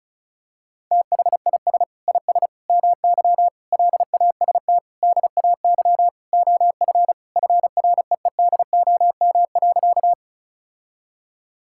Morse code, heard as a keyed tone, about 36 wpm, 700 Hz. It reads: THIS IS MY LAST DAY OF FREEDOM.
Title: Sentences from Top 1000 Words - Review Part 5 - 35wpm